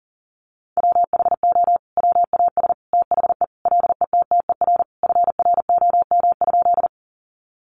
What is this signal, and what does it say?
Morse code, heard as a keyed tone, about 40 wpm, 700 Hz. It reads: WHO WAS THE LETTER FROM?
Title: Sentences from Top 300 Words 40wpm